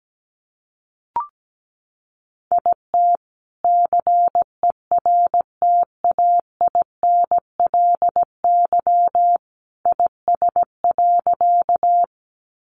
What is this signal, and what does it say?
Morse code, heard as a keyed tone, about 17 wpm, 700 Hz. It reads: IT CERTAINLY IS.
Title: Sentences from Top 900 Words 17wpm